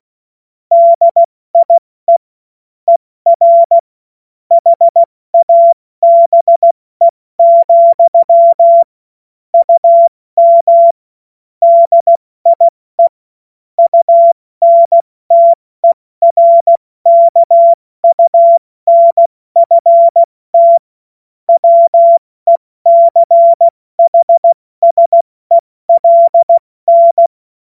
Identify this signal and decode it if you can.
Morse code, heard as a keyed tone, about 16 wpm, 680 Hz. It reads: DIE ER HABE, UM DIE UNTERKUNFT WECHSELN
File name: Morsekurs / Witze / Witz 25. Oktober 2022 70 F80